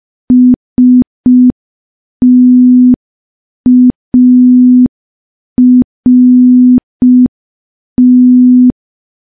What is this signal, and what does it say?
Morse code, heard as a keyed tone, about 5 wpm, 250 Hz. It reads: START
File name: morse / morse-training-05WPM